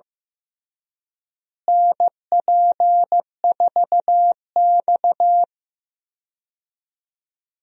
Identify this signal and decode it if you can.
Morse code, heard as a keyed tone, about 15 wpm, 700 Hz. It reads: NP4X
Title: USA Call Signs - 4 Characters - Repeated 3x - 15wpm